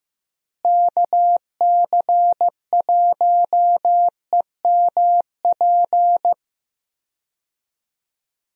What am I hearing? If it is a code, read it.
Morse code, heard as a keyed tone, about 16 wpm, 700 Hz. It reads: KC1EMP
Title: USA Call Signs - 6 Characters - Repeated 2x - 15wpm